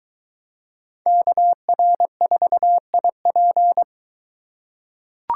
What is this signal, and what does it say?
Morse code, heard as a keyed tone, about 22 wpm, 700 Hz. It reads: KR4IP